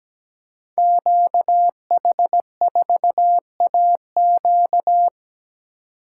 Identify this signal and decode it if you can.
Morse code, heard as a keyed tone, about 17 wpm, 700 Hz. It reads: QH4AQ
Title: Code Groups - 5 Letter:Numbers - 17wpm